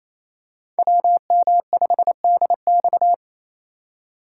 Morse code, heard as a keyed tone, about 28 wpm, 700 Hz. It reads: WM5DX
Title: K1USN SST QSO Practice - 28wpm